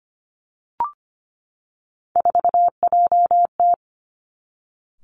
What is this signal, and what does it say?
Morse code, heard as a keyed tone, about 25 wpm, 700 Hz. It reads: 4JT
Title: Three Challenging Letter-Number Combinations 25wpm